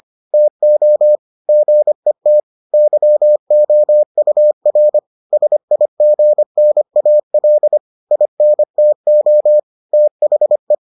Morse code, heard as a keyed tone, about 25 wpm, 600 Hz. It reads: TO GET YOUR SIGNAL INTO THE